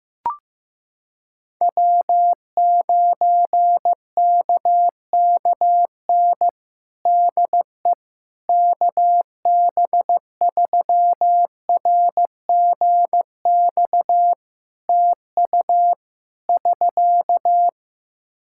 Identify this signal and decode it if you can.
Morse code, heard as a keyed tone, about 15 wpm, 700 Hz. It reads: W9KKN DE KB3RGX TU <SK>
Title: Practice QSO - Exchange 4 - Ending - 15wpm